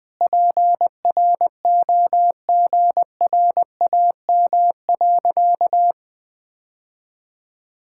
Morse code, heard as a keyed tone, about 20 wpm, 700 Hz. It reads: PROGRAM.
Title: Sentences from Top 1000 Words - Review Part 5 - 20wpm